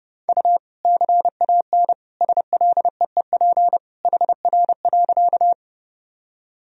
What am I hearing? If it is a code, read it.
Morse code, heard as a keyed tone, about 30 wpm, 700 Hz. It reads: U CAN SLEEP HR.
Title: Sentences from Top 500 Words and Abbreviations 30wpm